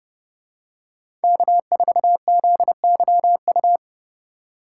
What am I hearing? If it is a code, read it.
Morse code, heard as a keyed tone, about 31 wpm, 700 Hz. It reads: K4ZYU
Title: USA Call Signs - 4x Character Spacing - 30wpm